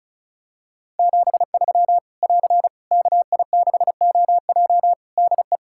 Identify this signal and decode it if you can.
Morse code, heard as a keyed tone, about 36 wpm, 700 Hz. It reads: 73 <AR> KI6OJ DE
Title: K1USN SST QSO Practice 35wpm